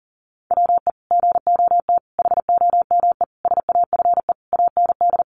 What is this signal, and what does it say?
Morse code, heard as a keyed tone, about 40 wpm, 700 Hz. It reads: WE GOT HOME SAFE AND
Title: Sentences from Top 900 Words 40wpm